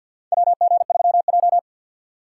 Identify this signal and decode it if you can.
Morse code, heard as a keyed tone, about 52 wpm, 700 Hz. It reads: WG3J
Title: USA Call Signs - 4 Characters - 3x Character Spacing - 50wpm